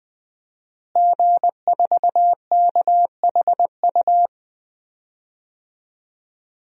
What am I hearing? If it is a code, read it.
Morse code, heard as a keyed tone, about 20 wpm, 700 Hz. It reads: G4KHU
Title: DXCC-100 Call Signs - 5 Characters - Repeated 3x - 20wpm